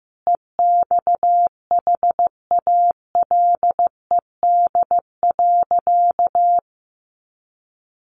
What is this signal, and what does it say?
Morse code, heard as a keyed tone, about 15 wpm, 700 Hz. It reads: EXHALED.
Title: A Sound Of Thunder 15wpm